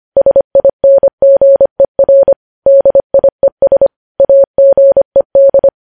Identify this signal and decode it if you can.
Morse code, heard as a keyed tone, about 25 wpm, 550 Hz. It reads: SINGER DIES AGED